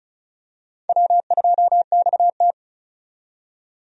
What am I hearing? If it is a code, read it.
Morse code, heard as a keyed tone, about 36 wpm, 700 Hz. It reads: W2XT
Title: USA Call Signs - 4 Characters - 35wpm